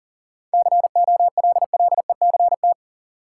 Morse code, heard as a keyed tone, about 38 wpm, 700 Hz. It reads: COLLECT
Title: Sentences from Top 1600 Words - New Word Drill Alt - 40wpm